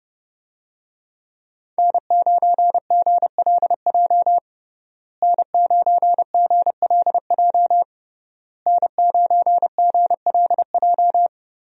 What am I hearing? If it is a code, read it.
Morse code, heard as a keyed tone, about 30 wpm, 700 Hz. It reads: N9GLJ N9GLJ N9GLJ
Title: USA Call Signs - 5 Characters - Repeated 3x - 30wpm